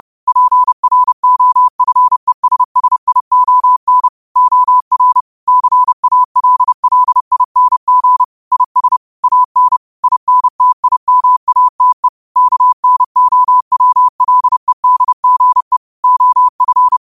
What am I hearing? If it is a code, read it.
Morse code, heard as a keyed tone, about 30 wpm, 1000 Hz. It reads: PROFESSION OR CALLING IS AN INTIMATE KNOWLEDGE OF
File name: qotdfiles / 30wpm